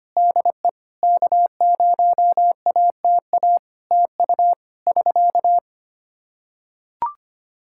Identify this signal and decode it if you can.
Morse code, heard as a keyed tone, about 25 wpm, 700 Hz. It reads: DE K0ATA TU <SK>
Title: Practice QSO - Exchange 4 - Ending - 25wpm